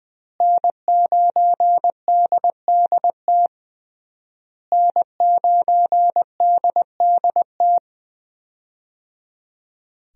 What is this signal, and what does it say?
Morse code, heard as a keyed tone, about 20 wpm, 700 Hz. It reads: N9DDT N9DDT
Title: USA Call Signs - 5 Characters - Repeated 3x - 20wpm